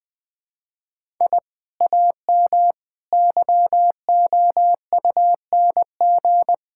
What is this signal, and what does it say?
Morse code, heard as a keyed tone, about 20 wpm, 700 Hz. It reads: I AM YOUNG
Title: Sentences from Top 1000 Words - Review Part 6 - 20wpm